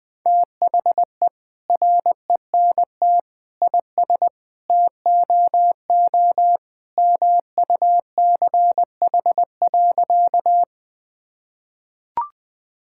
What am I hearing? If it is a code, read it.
Morse code, heard as a keyed tone, about 21 wpm, 700 Hz. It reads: THE RENT IS TOO MUCH.
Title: Sentences from Top 1000 Words - Review Part 5 - 20wpm